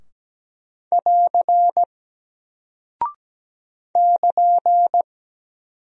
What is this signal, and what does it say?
Morse code, heard as a keyed tone, about 17 wpm, 700 Hz. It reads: <AR> <KN>
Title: Single Character (Letter:Number:Punctuation:Prosign) Alt 17wpm